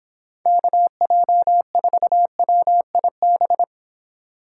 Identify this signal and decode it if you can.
Morse code, heard as a keyed tone, about 26 wpm, 700 Hz. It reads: KJ4WIB